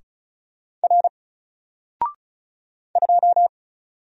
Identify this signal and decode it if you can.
Morse code, heard as a keyed tone, about 36 wpm, 700 Hz. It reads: R 2
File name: Single-Letter-Number-35wpm